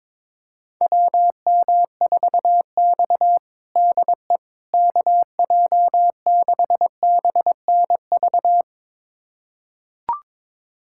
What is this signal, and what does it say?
Morse code, heard as a keyed tone, about 22 wpm, 700 Hz. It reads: WM4X DE KJ6BNV
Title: USA Callsign DE Callsign 22wpm